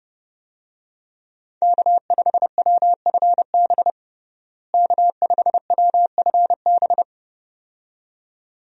Morse code, heard as a keyed tone, about 30 wpm, 700 Hz. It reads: K5WFB K5WFB
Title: USA Call Signs - 5 Characters - Repeated 2x - 30wpm